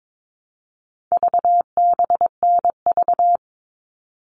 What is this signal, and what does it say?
Morse code, heard as a keyed tone, about 22 wpm, 700 Hz. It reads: VBNV
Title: All Cut Numbers - 4-Digit 22wpm